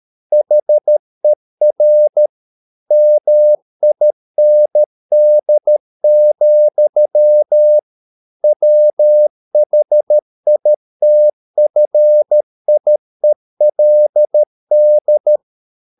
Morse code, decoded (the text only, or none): HER MIND, WHITFIELD